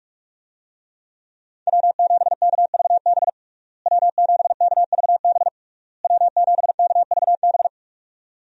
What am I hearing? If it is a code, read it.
Morse code, heard as a keyed tone, about 46 wpm, 700 Hz. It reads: W7KVB W7KVB W7KVB
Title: USA Call Signs - 5 Characters - Repeated 3x - 45wpm